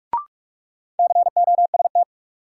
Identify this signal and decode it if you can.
Morse code, heard as a keyed tone, about 45 wpm, 700 Hz. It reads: COST
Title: Sets of 3 Words - New Word Drill - 45wpm